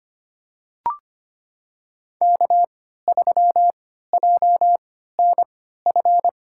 Morse code, heard as a keyed tone, about 25 wpm, spaced as for 14 wpm, 700 Hz. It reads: K3JNF
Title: USA Call Signs - 5 Characters - 3x Character Spacing - 25wpm